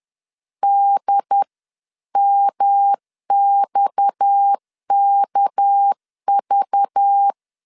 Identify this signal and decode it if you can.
Morse code, heard as a keyed tone, about 10 wpm, 800 Hz. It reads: DMXKV
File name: LEZIONI / 8 YJ45 / 8 YJ 10wpm 2651